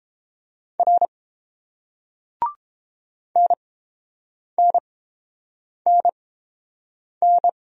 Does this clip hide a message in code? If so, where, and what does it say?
Yes: Morse code from 0.8 to 7.5 s, heard as a keyed tone, about 30 wpm, spaced as for 7 wpm, 700 Hz. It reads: R NNNN